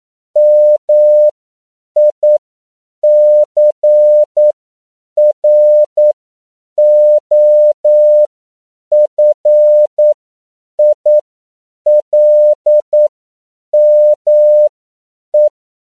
Morse code, heard as a keyed tone, about 9 wpm, spaced as for 7 wpm, 600 Hz. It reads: MICROFILME